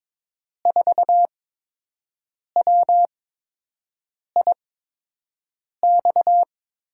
Morse code, heard as a keyed tone, about 22 wpm, spaced as for 6 wpm, 700 Hz. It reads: 4WIX
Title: USA Call Signs - 8x Character Spacing - 22wpm